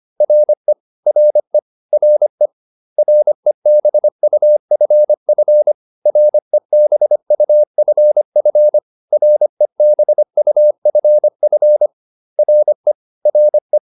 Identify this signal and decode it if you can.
Morse code, heard as a keyed tone, about 25 wpm, 600 Hz. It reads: RE RE RE REBUFF REBUFF REBUFF RE RE